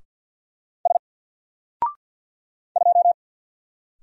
Morse code, heard as a keyed tone, about 49 wpm, 700 Hz. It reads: S 2
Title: Single Character (Letter:Number:Punctuation:Prosign) Alt 50wpm